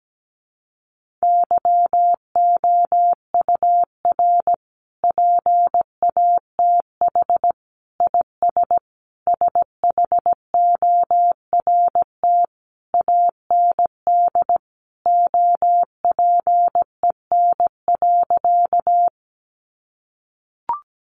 Morse code, heard as a keyed tone, about 17 wpm, 700 Hz. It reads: YOUR PATH IS SHORT AND OPEN.
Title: Sentences from Top 1400 Words 17wpm